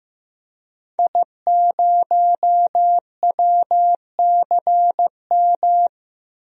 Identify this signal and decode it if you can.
Morse code, heard as a keyed tone, about 15 wpm, 700 Hz. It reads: I0WCM
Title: DXCC-100 Call Signs - 5 Characters - 15wpm